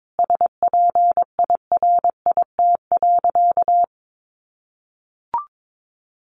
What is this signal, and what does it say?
Morse code, heard as a keyed tone, about 22 wpm, 700 Hz. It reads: SPIRIT.
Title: Sentences from Top 1400 Words 22wpm